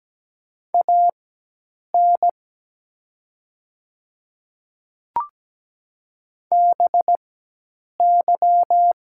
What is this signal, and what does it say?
Morse code, heard as a keyed tone, about 17 wpm, spaced as for 8 wpm, 700 Hz. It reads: AN BY
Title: 4x Character Spacing - Top 300 Words - 17wpm